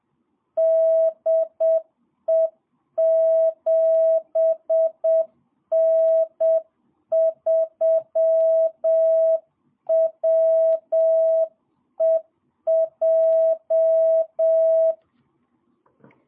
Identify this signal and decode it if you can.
Morse code, heard as a keyed tone, about 7 wpm, 650 Hz. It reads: DE 7N3WEJ
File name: jh8cbh / 2022cw